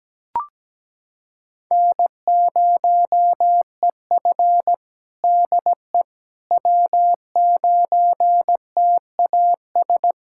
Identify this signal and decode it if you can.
Morse code, heard as a keyed tone, about 17 wpm, 700 Hz. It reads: N0EF DE W9TAS
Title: Practice QSO - Exchange 4 - Ending - 17wpm